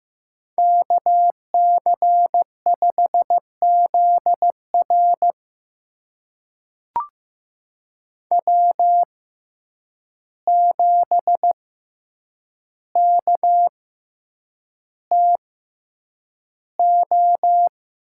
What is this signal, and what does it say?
Morse code, heard as a keyed tone, about 15 wpm, spaced as for 8 wpm, 700 Hz. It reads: KC5ZR W7KTO
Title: USA Call Signs - 6x Character Spacing - 15wpm